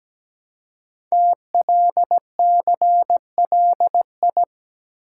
Morse code, heard as a keyed tone, about 17 wpm, 700 Hz. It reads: TLCLI
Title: Code Groups - 5 Letter:Numbers - 17wpm